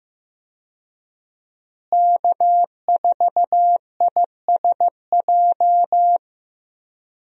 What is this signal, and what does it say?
Morse code, heard as a keyed tone, about 15 wpm, 700 Hz. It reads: K4ISJ
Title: USA Call Signs - 5 Characters - Repeated 2x - 15wpm